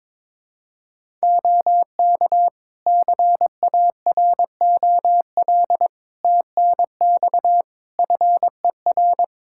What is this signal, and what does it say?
Morse code, heard as a keyed tone, about 22 wpm, 700 Hz. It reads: OK CAROL TNX FER
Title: Practice QSO - Exchange 4 - Ending - 22wpm